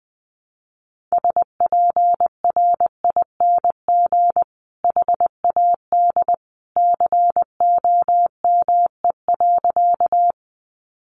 Morse code, heard as a keyed tone, about 20 wpm, 700 Hz. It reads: SPRING HAD COME.
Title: Sentences from Top 1200 Words 20wpm